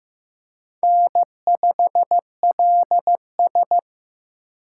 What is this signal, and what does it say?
Morse code, heard as a keyed tone, about 15 wpm, 700 Hz. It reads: N5LS
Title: USA Call Signs - 4 Characters - 6x Character Spacing - 15wpm